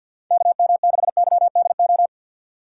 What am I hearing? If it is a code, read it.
Morse code, heard as a keyed tone, about 52 wpm, 700 Hz. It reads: KM6YDO
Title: USA Call Signs - 6 Characters - 3x Character Spacing - 50wpm